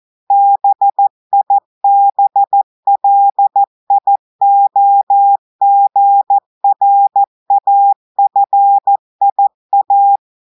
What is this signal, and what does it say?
Morse code, heard as a keyed tone, about 14 wpm, 800 Hz. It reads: BIBLIOGRAFIA
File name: audios / portugais / portugais texte en clair 1 vitesse 14 mots minutes